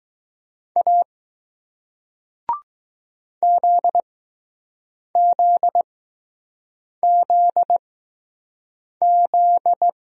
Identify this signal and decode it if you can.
Morse code, heard as a keyed tone, about 20 wpm, spaced as for 6 wpm, 700 Hz. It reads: A ZZZZ